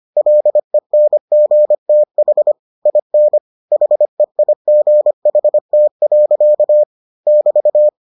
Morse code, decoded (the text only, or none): LENGTH IN HEIGHT. <BT>